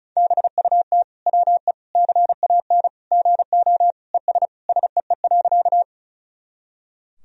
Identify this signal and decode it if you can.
Morse code, heard as a keyed tone, about 34 wpm, 700 Hz. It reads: BUT WE CAN GO ES SEE.
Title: Sentences from Top 100 Words and Abbreviations 35wpm